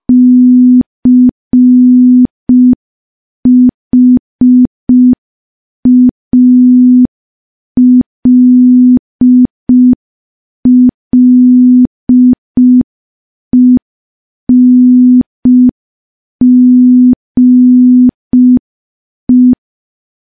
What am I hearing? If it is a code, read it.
Morse code, heard as a keyed tone, about 5 wpm, 250 Hz. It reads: CHALLENGE
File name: morse / morse-training-05WPM